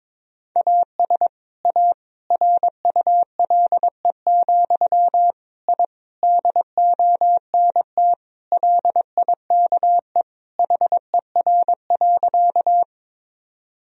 Morse code, heard as a keyed tone, about 22 wpm, 700 Hz. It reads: AS A RULE, I DONT LIKE HER.
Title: Sentences from Top 1000 Words - Review Part 2 - 22wpm